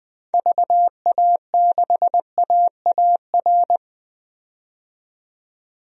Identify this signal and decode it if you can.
Morse code, heard as a keyed tone, about 20 wpm, 700 Hz. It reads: VA6AAR
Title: DXCC-100 Call Signs - 6 Characters - 20wpm